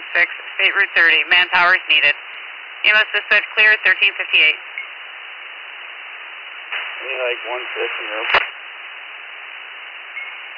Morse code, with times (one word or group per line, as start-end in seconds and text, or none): none